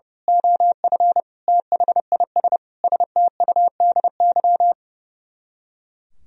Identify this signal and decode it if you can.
Morse code, heard as a keyed tone, about 29 wpm, 700 Hz. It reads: OF THIS STUDY
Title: Speed-Racing - Sets of 5 Words 30wpm